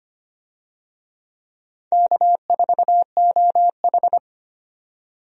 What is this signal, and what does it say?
Morse code, heard as a keyed tone, about 25 wpm, 700 Hz. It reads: K4OH
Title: USA Call Signs - 4 Characters - 25wpm